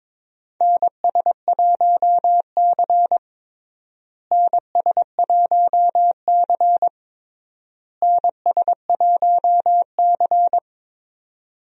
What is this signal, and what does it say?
Morse code, heard as a keyed tone, about 22 wpm, 700 Hz. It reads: NS1C NS1C NS1C
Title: USA Call Signs - 4 Characters - Repeated 3x - 22wpm